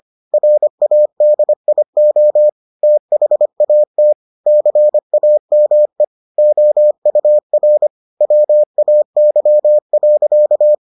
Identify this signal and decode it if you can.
Morse code, heard as a keyed tone, about 25 wpm, 600 Hz. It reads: RADIO THAT CAME OUR WAY.